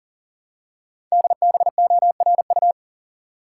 Morse code, heard as a keyed tone, about 40 wpm, 700 Hz. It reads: DBORU
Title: Code Groups - 5 Letters - 40wpm